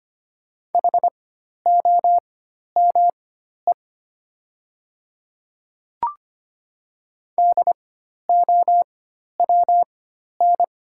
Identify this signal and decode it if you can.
Morse code, heard as a keyed tone, about 25 wpm, spaced as for 12 wpm, 700 Hz. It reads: HOME DOWN
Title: 4x Character Spacing - Top 200 Words - 25wpm